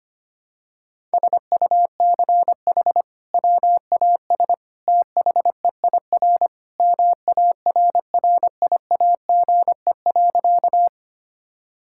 Morse code, heard as a keyed tone, about 25 wpm, 700 Hz. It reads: SUCH WAS THEIR MARRIAGE.
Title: Sentences from Top 1000 Words - Review Part 5 - 25wpm